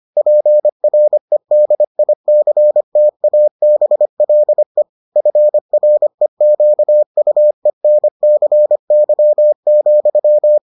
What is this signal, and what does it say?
Morse code, heard as a keyed tone, about 25 wpm, 600 Hz. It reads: PREDICTABLE FREQUENCY,